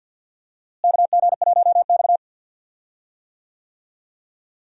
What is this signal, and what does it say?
Morse code, heard as a keyed tone, about 49 wpm, 700 Hz. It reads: KG1X
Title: USA Call Signs - 4 Characters - Repeated 2x - 50wpm